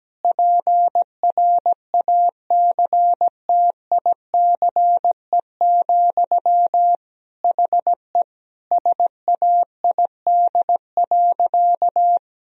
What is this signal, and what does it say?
Morse code, heard as a keyed tone, about 17 wpm, 700 Hz. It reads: PRACTICE, HE SAID.
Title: Sentences from Top 1000 Words - Review Part 4 - 17wpm